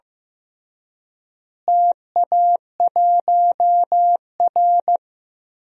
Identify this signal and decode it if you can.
Morse code, heard as a keyed tone, about 15 wpm, 700 Hz. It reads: TA1R
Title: DXCC-100 Call Signs - 4 Characters - Repeated 2x - 15wpm